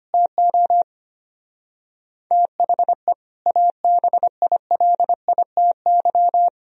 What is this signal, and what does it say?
Morse code, heard as a keyed tone, about 33 wpm, 700 Hz. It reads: TO THE ABILITY